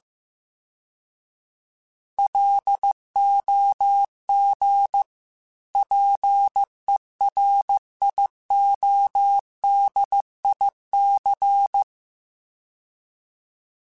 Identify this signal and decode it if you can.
Morse code, heard as a keyed tone, about 15 wpm, 800 Hz. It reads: LOG PERIODIC